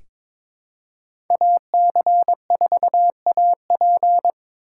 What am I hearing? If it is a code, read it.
Morse code, heard as a keyed tone, about 21 wpm, 700 Hz. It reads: AC4AP
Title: USA Call Signs - 5 Characters - 2x Character Spacing - 22wpm